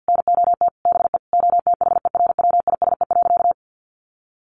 Morse code, heard as a keyed tone, about 50 wpm, 700 Hz. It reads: NOT BE OTHERWISE.
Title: Sentences from Top 1200 Words 50wpm